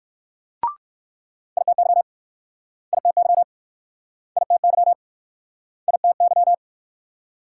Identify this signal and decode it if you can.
Morse code, heard as a keyed tone, about 62 wpm, 700 Hz. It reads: ITY ITY ITY ITY